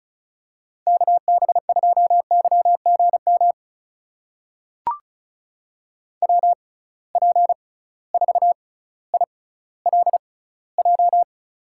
Morse code, heard as a keyed tone, about 36 wpm, spaced as for 18 wpm, 700 Hz. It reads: KB2YGM WP4ILJ
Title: USA Call Signs - 6 Characters - 6x Character Spacing - 35wpm